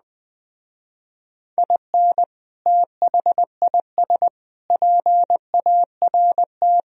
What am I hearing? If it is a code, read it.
Morse code, heard as a keyed tone, about 21 wpm, 700 Hz. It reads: IN THIS PART
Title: Speed-Racing - Sets of 5 Words 17wpm